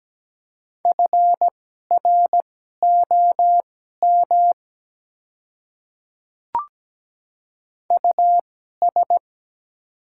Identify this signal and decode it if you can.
Morse code, heard as a keyed tone, about 17 wpm, spaced as for 12 wpm, 700 Hz. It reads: FROM US
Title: 2x Character Spacing - Top 300 Words - 17wpm